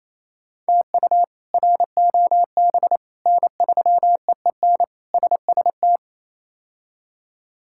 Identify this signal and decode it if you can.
Morse code, heard as a keyed tone, about 28 wpm, 700 Hz. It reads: TU ROB N3EEN SST
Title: K1USN SST QSO Practice - 28wpm